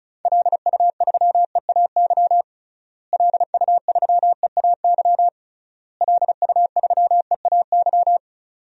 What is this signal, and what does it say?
Morse code, heard as a keyed tone, about 36 wpm, 700 Hz. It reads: LU3EAY LU3EAY LU3EAY
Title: DXCC-100 Call Signs - 6 Characters - Repeated 3x - 35wpm